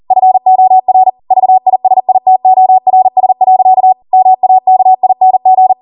Morse code, heard as a keyed tone, about 40 wpm, 750 Hz. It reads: FOR VISITORS. MAKING